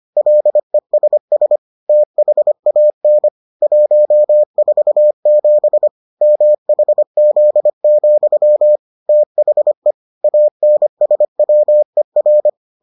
Morse code, heard as a keyed tone, about 25 wpm, 600 Hz. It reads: LESS THAN 147 MHZ, THE ANSWER